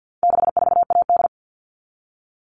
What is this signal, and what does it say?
Morse code, heard as a keyed tone, about 50 wpm, 700 Hz. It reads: 64AD
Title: All Cut Numbers - 4-Digit 50wpm